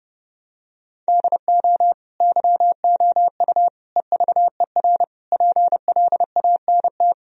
Morse code, heard as a keyed tone, about 30 wpm, 700 Hz. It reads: DO YOU EVER PLANT
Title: Sentences from Top 1000 Words - Review Part 1 - 30wpm